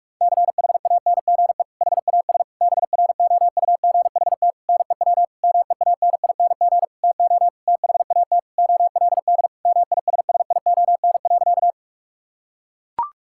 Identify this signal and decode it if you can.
Morse code, heard as a keyed tone, about 41 wpm, 700 Hz. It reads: CHANGE HAS BROUGHT NEW MEANING TO THAT OLD MISSION.
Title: Sentences from Top 1600 Words 45wpm